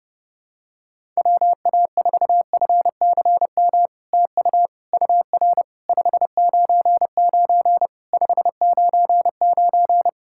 Morse code, heard as a keyed tone, about 30 wpm, 700 Hz. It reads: WA4FCM TU UR 599 599
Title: SOTA QSO Practice 30wpm